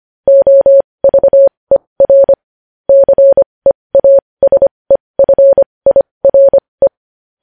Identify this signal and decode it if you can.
Morse code, heard as a keyed tone, about 25 wpm, spaced as for 20 wpm, 550 Hz. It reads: OVER CEASEFIRE
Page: news headlines in morse code at 20 wpm